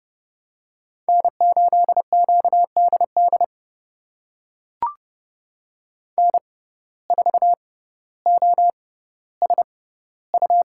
Morse code, heard as a keyed tone, about 30 wpm, spaced as for 15 wpm, 700 Hz. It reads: N8QDD N4OSU
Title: USA Call Signs - 5 Characters - 6x Character Spacing - 30wpm